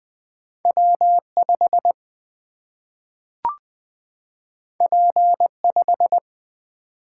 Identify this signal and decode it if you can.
Morse code, heard as a keyed tone, about 20 wpm, 700 Hz. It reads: W5 P5
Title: Single-Letter Single-Number 20wpm